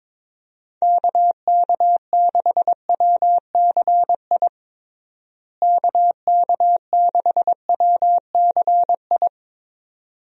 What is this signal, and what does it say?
Morse code, heard as a keyed tone, about 22 wpm, 700 Hz. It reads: KK6WCI KK6WCI
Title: USA Call Signs - 6 Characters - Repeated 2x - 22wpm